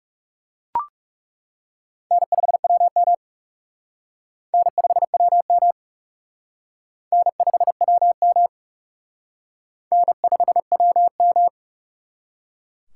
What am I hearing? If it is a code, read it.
Morse code, heard as a keyed tone, about 38 wpm, 700 Hz. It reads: N5WM N5WM N5WM N5WM
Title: Speed Racing - USA 1x2 Call Signs 30wpm